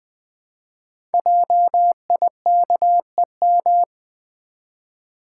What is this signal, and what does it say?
Morse code, heard as a keyed tone, about 20 wpm, 700 Hz. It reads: JIKEM
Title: Code Groups - 5 Letters - 20wpm